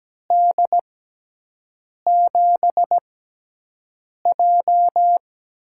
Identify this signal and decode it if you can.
Morse code, heard as a keyed tone, about 17 wpm, spaced as for 6 wpm, 700 Hz. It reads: D7J
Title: USA Call Signs - 6 Characters - 6x Character Spacing - 17wpm